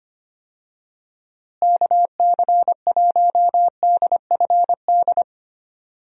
Morse code, heard as a keyed tone, about 25 wpm, 700 Hz. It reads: KC1DFD
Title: USA Call Signs - 6 Characters - Repeated 2x - 25wpm